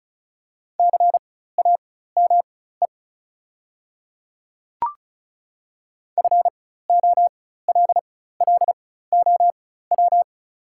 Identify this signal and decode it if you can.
Morse code, heard as a keyed tone, about 36 wpm, spaced as for 16 wpm, 700 Hz. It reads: CAME FOLLOW
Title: 4x Character Spacing - Top 200 Words - 35wpm